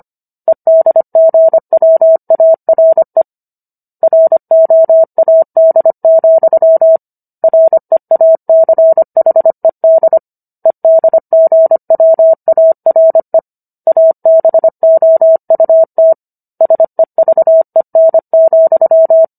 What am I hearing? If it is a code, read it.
Morse code, heard as a keyed tone, about 25 wpm, 650 Hz. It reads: EDGWARE ROAD, REACHED EDGWARE ABOUT SEVEN,